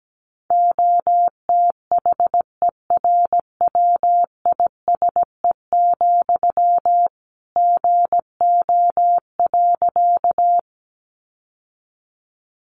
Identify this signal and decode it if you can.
Morse code, heard as a keyed tone, about 17 wpm, 700 Hz. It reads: OTHERWISE, GO.
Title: Sentences from Top 1200 Words 17wpm